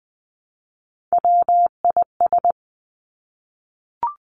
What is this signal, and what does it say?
Morse code, heard as a keyed tone, about 20 wpm, 700 Hz. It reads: WIS
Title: Three Challenging Letter-Number Combinations 20wpm